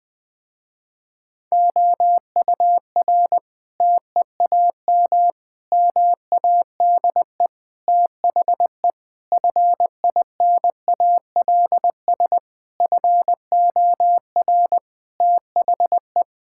Morse code, heard as a keyed tone, about 20 wpm, 700 Hz. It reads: OUR TEAM MADE THE FINALS FOR THE
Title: Sentences with 7-Word Phrases - Spoken Beginning - 20wpm